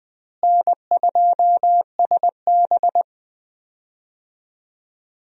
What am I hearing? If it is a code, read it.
Morse code, heard as a keyed tone, about 20 wpm, 700 Hz. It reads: N2SB
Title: USA Call Signs - 4 Characters - Repeated 2x - 20wpm